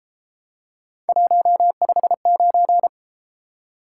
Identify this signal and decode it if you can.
Morse code, heard as a keyed tone, about 31 wpm, 700 Hz. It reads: 159